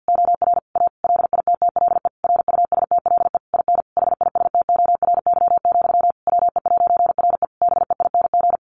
Morse code, heard as a keyed tone, about 50 wpm, 700 Hz. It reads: OR A LITTLE RUSTLE IN HISTORY, WE'RE BEING
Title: A Sound Of Thunder 50wpm